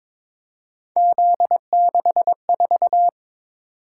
Morse code, heard as a keyed tone, about 22 wpm, 700 Hz. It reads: Z64
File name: practice / 22wpm / Three Letter-Number Combinations - 22wpm